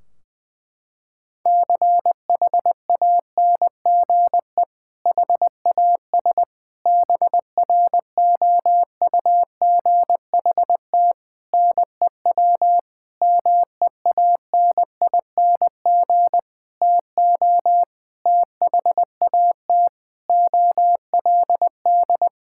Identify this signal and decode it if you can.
Morse code, heard as a keyed tone, about 20 wpm, 700 Hz. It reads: CHANGE HAS BROUGHT NEW MEANING TO THAT OLD
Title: Sentences from Top 1600 Words 20wpm